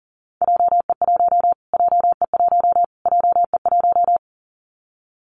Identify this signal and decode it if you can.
Morse code, heard as a keyed tone, about 40 wpm, 700 Hz. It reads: JE1 JE1 JE1
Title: Callsign Prefixes for Top 25 DXCC - Repeated 3x - 40wpm